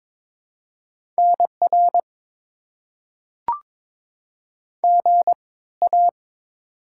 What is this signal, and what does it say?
Morse code, heard as a keyed tone, about 21 wpm, spaced as for 16 wpm, 690 Hz. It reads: NR GA
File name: Top 90 QSO Elements - 3x Character Spacing - 22wpm